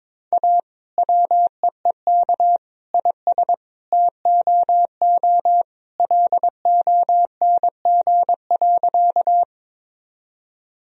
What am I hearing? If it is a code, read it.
Morse code, heard as a keyed tone, about 22 wpm, 700 Hz. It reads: A WEEK IS TOO LONG.
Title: Sentences from Top 1000 Words - Review Part 2 - 22wpm